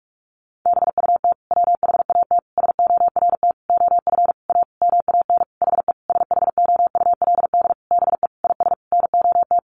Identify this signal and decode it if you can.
Morse code, heard as a keyed tone, about 44 wpm, 700 Hz. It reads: BUT WHAT SORT OF A MAN HE SHOULD BE IS NOT
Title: Sentences from Top 1300 Words 45wpm